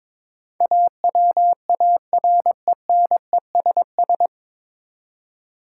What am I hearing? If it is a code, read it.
Morse code, heard as a keyed tone, about 22 wpm, 700 Hz. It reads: AWARENESS
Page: Morse Code Ninja practice 22wpm